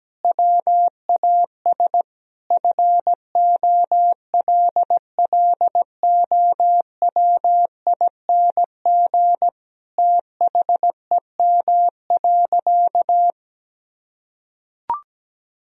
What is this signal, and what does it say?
Morse code, heard as a keyed tone, about 17 wpm, 700 Hz. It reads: WAS FOLLOWING THEM.
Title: Sentences from Top 1000 Words - Review Part 4 - 17wpm